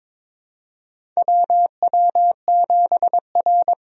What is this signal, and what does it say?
Morse code, heard as a keyed tone, about 22 wpm, 700 Hz. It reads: WW7R
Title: USA Call Signs - 4 Characters - 22wpm